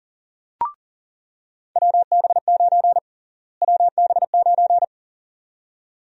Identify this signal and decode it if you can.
Morse code, heard as a keyed tone, about 40 wpm, 700 Hz. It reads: WB9 WB9
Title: USA Call Sign Prefixes - Repeated 2x - 40wpm